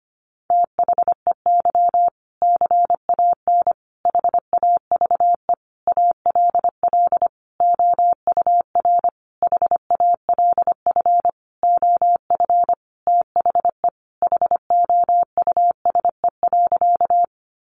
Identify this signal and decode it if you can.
Morse code, heard as a keyed tone, about 25 wpm, 700 Hz. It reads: THEY CAN HAVE ALL OUR HALF OF THE HOUSE.
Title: Sentences from Top 400 Words 25wpm